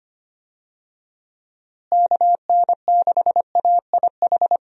Morse code, heard as a keyed tone, about 25 wpm, 700 Hz. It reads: KN6AIH
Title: USA Call Signs - 6 Characters - Repeated 2x - 25wpm